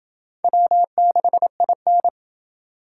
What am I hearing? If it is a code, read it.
Morse code, heard as a keyed tone, about 27 wpm, 700 Hz. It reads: W6IN